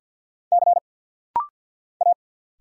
Morse code, heard as a keyed tone, about 50 wpm, 690 Hz. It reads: C A
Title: Single Letter-Number - Rapid-Fire 50wpm